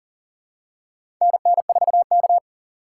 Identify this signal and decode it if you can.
Morse code, heard as a keyed tone, about 38 wpm, 690 Hz. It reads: NN4K
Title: USA Call Signs - 4 Characters - 2x Character Spacing - 40wpm